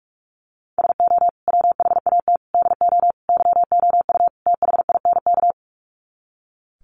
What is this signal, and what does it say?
Morse code, heard as a keyed tone, about 44 wpm, 700 Hz. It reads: SO WHAT DO YOU THINK
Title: Sets of 5 Words Encore2 45wpm